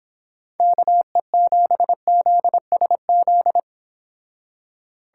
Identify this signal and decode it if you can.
Morse code, heard as a keyed tone, about 26 wpm, 700 Hz. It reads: KE7ZSZ